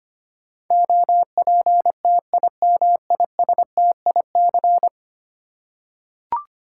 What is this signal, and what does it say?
Morse code, heard as a keyed tone, about 26 wpm, 700 Hz. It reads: OPTIMISTIC
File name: 10 Letter Words - First 3 Letters Given - 25wpm